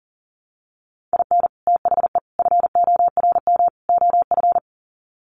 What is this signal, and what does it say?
Morse code, heard as a keyed tone, about 37 wpm, 700 Hz. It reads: IN THE FORM OF
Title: Sets of 4 Words 40wpm